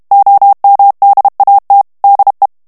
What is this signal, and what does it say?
Morse code, heard as a keyed tone, about 32 wpm, 800 Hz. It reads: OMDAT DE
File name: HOLLANDAIS texte clair 1 vitesse 30 mots minute